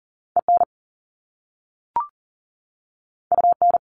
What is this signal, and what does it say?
Morse code, heard as a keyed tone, about 39 wpm, 700 Hz. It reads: EN UN
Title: Top Two Letter Combinations in Words 40wpm